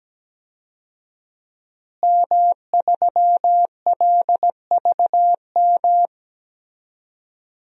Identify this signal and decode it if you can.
Morse code, heard as a keyed tone, about 17 wpm, 700 Hz. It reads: M3LVM
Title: DXCC-100 Call Signs - 5 Characters - Repeated 2x - 17wpm